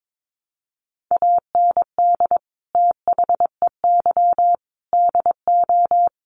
Morse code, heard as a keyed tone, about 22 wpm, 700 Hz. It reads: AND THEY DO
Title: Sets of 5 Words Encore2 22wpm